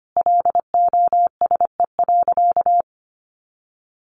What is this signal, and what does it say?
Morse code, heard as a keyed tone, about 25 wpm, 700 Hz. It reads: LOSE.
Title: Sentences from Top 1100 Words 25wpm